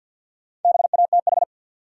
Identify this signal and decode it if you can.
Morse code, heard as a keyed tone, about 52 wpm, 700 Hz. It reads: BATH
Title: Sentences from Top 1600 Words - New Word Drill - 50wpm